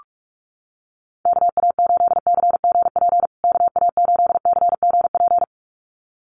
Morse code, heard as a keyed tone, about 44 wpm, 700 Hz. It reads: KA8CGP KA8CGP
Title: USA Call Signs Repeated 2x 45wpm